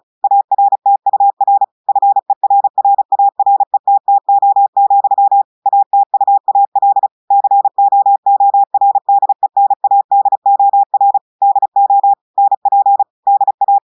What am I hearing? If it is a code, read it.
Morse code, heard as a keyed tone, about 36 wpm, 800 Hz. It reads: ARTUR FERRARETTO, ATUAL COORDENADOR DO NP DA
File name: audios / portugais / portugais texte en clair 1 vitesse 35 mots minutes